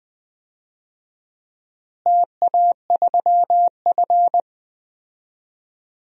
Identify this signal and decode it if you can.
Morse code, heard as a keyed tone, about 20 wpm, 700 Hz. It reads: TA3F
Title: DXCC-100 Call Signs - 4 Characters - Repeated 2x - 20wpm